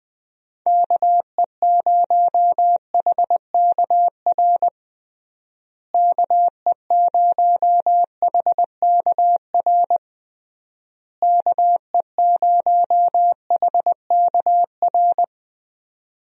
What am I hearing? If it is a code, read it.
Morse code, heard as a keyed tone, about 20 wpm, 700 Hz. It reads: KE0HKR KE0HKR KE0HKR
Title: USA Call Signs - 6 Characters - Repeated 3x - 20wpm